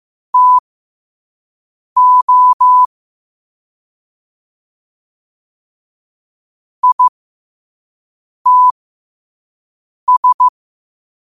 Morse code, heard as a keyed tone, about 15 wpm, spaced as for 5 wpm, 1000 Hz. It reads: TO ITS
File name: qotdfiles / 5wpm